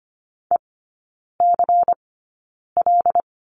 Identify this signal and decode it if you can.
Morse code, heard as a keyed tone, about 25 wpm, spaced as for 9 wpm, 700 Hz. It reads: ECL
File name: course / 25:10wpm / 093 Lesson 16e - Letter c - Three Characters 25wpm